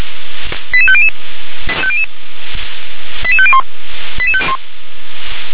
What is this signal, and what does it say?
Five-tone selective call (ZVEI), heard as quick runs of five tones, tones of about 70 ms. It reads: 88500 88501